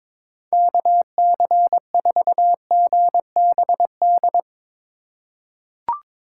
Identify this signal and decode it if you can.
Morse code, heard as a keyed tone, about 21 wpm, 700 Hz. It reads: KC4GBD
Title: USA Call Signs - 2x Character Spacing - 22wpm